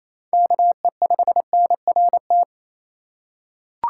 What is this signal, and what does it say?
Morse code, heard as a keyed tone, about 28 wpm, 700 Hz. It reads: KE5NRT
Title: USA Call Signs - 6 Characters - 6x Character Spacing - 28wpm